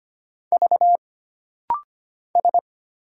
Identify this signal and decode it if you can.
Morse code, heard as a keyed tone, about 25 wpm, 700 Hz. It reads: V S
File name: Single Character (Letter:Number:Punctuation:Prosign) - Rapid-Fire - 25wpm